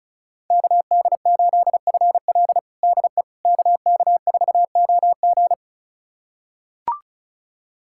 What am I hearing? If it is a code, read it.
Morse code, heard as a keyed tone, about 34 wpm, 700 Hz. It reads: KD8FL DE KK4OG
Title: USA Callsign DE Callsign - 5 Characters - 35wpm